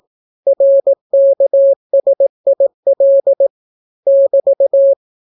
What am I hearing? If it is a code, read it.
Morse code, heard as a keyed tone, about 18 wpm, 550 Hz. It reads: RKSIL <BT>